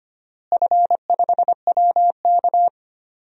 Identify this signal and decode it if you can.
Morse code, heard as a keyed tone, about 25 wpm, 700 Hz. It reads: F5WK
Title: DXCC-100 Call Signs - 4 Characters - Repeated 2x - 25wpm